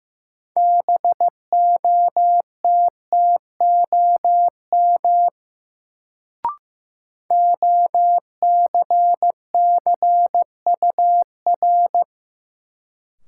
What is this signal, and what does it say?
Morse code, heard as a keyed tone, about 15 wpm, 700 Hz. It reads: BOTTOM OCCUR
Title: Sentences from Top 900 Words - New Word Drill - 15wpm